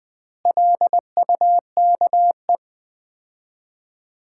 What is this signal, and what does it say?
Morse code, heard as a keyed tone, about 20 wpm, 700 Hz. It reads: LUKE
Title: Common QSO Phrases - 4x Word-Spacing - 20wpm